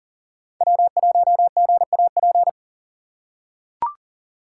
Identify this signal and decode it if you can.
Morse code, heard as a keyed tone, about 40 wpm, 700 Hz. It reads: W1GAP
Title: USA Call Signs - 5 Characters - 3x Character Spacing - 40wpm